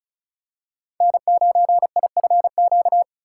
Morse code, heard as a keyed tone, about 36 wpm, 700 Hz. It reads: N9IFQ
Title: USA Call Signs - 5 Characters - 4x Character Spacing - 35wpm